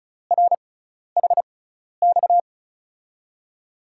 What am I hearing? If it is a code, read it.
Morse code, heard as a keyed tone, about 36 wpm, spaced as for 12 wpm, 700 Hz. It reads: RHX
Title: USA Call Signs - 5 Characters - 6x Character Spacing - 35wpm